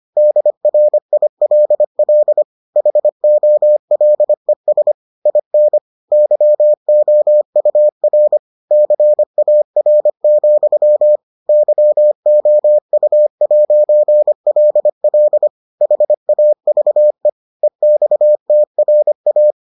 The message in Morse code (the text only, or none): DRILL HOLES IN YOUR CAR, YOU'LL HAVE EXTRA